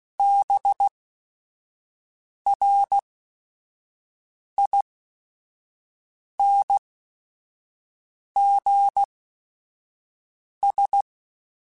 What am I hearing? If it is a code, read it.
Morse code, heard as a keyed tone, about 16 wpm, spaced as for 5 wpm, 800 Hz. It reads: BRINGS
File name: Quotes / Quotes-mp3s / Persperation @5